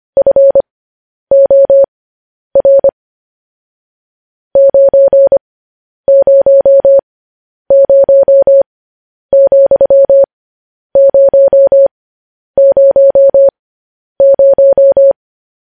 Morse code, heard as a keyed tone, about 25 wpm, spaced as for 10 wpm, 550 Hz. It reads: FOR 900,000
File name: podcast / News headlines-1x-25x10